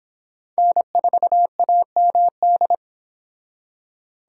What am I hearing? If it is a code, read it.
Morse code, heard as a keyed tone, about 25 wpm, 700 Hz. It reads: N4AMD